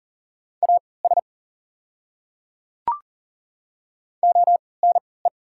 Morse code, heard as a keyed tone, about 38 wpm, spaced as for 22 wpm, 700 Hz. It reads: AS ONE
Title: 3x Character Spacing - Top 200 Words - 40wpm